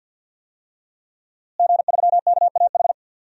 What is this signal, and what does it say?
Morse code, heard as a keyed tone, about 47 wpm, 700 Hz. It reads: G3KAH
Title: DXCC-100 Call Signs - 5 Characters - Repeated 2x - 50wpm